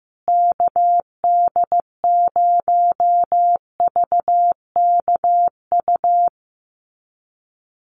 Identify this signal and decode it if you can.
Morse code, heard as a keyed tone, about 15 wpm, 700 Hz. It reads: KD0VKU
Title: USA Call Signs Repeated 2x 15wpm